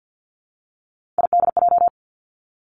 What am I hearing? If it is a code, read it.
Morse code, heard as a keyed tone, about 50 wpm, 700 Hz. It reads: IDJ
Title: Three Challenging Letter-Number Combinations 50wpm